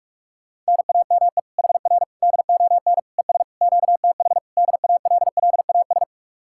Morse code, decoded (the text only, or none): NAME HR DON ES QTH DALLAS